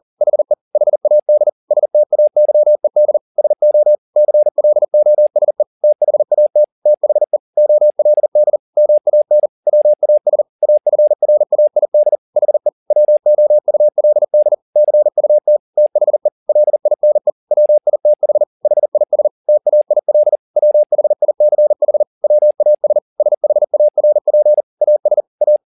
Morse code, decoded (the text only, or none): HE HAD STAYED SO CLOSE THAT THE OLD MAN WAS AFRAID HE WOULD CUT THE LINE WITH HIS TAIL WHICH WAS SHARP AS A